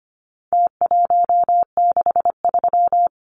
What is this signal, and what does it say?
Morse code, heard as a keyed tone, about 25 wpm, 700 Hz. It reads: T163
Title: Numbers - Contest Style - 4-Digit 25wpm